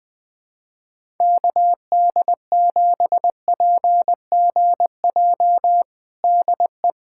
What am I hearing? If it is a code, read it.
Morse code, heard as a keyed tone, about 21 wpm, 700 Hz. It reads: KD7PGJ DE
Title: USA Callsign DE Callsign 20wpm